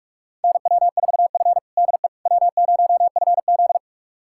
Morse code, heard as a keyed tone, about 46 wpm, 700 Hz. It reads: NW4F DE W0FZ
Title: USA Callsign DE Callsign - 4 Characters - 45wpm